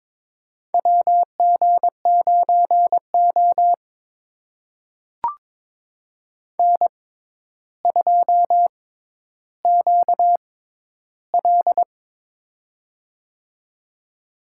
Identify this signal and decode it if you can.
Morse code, heard as a keyed tone, about 21 wpm, spaced as for 11 wpm, 700 Hz. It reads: WG9O N2QL
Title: USA Call Signs - 4 Characters - 6x Character Spacing - 22wpm